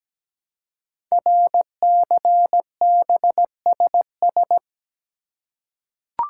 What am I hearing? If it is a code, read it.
Morse code, heard as a keyed tone, about 17 wpm, 700 Hz. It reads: RCBSS
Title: Code Groups - 5 Letters - 17wpm